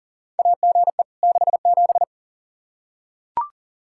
Morse code, heard as a keyed tone, about 40 wpm, 700 Hz. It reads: AGE 67
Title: Practice QSO - Exchange 3 - More Info - 40wpm